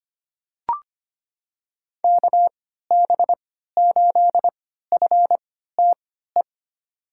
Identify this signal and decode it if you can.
Morse code, heard as a keyed tone, about 25 wpm, spaced as for 14 wpm, 700 Hz. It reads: KB8FTE
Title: USA Call Signs - 6 Characters - 3x Character Spacing - 25wpm